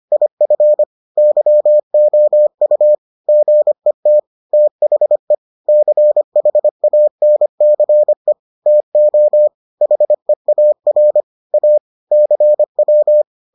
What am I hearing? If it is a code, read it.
Morse code, heard as a keyed tone, about 25 wpm, 600 Hz. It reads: IF YOU GET THE CHANCE TO HEAR A CW